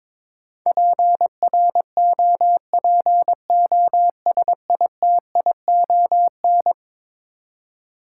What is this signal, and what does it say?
Morse code, heard as a keyed tone, about 21 wpm, 700 Hz. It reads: PROPOSITION
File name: practice / 22wpm / Eleven Letter Words - 22wpm